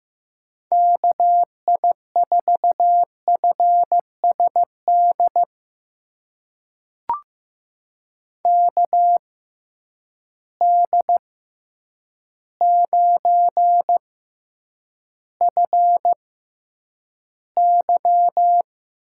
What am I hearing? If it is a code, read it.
Morse code, heard as a keyed tone, about 15 wpm, spaced as for 8 wpm, 700 Hz. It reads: KI4FSD KD9FY
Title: USA Call Signs - 6 Characters - 6x Character Spacing - 15wpm